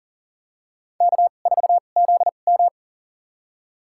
Morse code, heard as a keyed tone, about 40 wpm, spaced as for 29 wpm, 700 Hz. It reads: K4ZM
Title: USA Call Signs - 2x Character Spacing - 40wpm